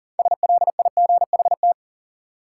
Morse code, heard as a keyed tone, about 40 wpm, 700 Hz. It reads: SLIGHT
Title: Six Letter Words - 40wpm